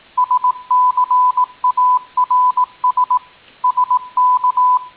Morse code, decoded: SCARS SK